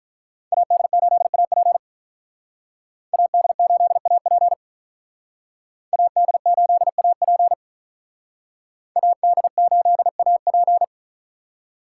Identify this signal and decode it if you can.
Morse code, heard as a keyed tone, about 45 wpm, 700 Hz. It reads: AD8AP AD8AP AD8AP AD8AP